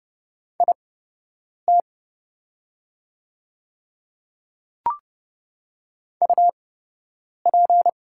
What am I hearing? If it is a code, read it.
Morse code, heard as a keyed tone, about 31 wpm, spaced as for 8 wpm, 700 Hz. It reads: IT UP
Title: 8x Character Spacing - Top 500 Words - 30wpm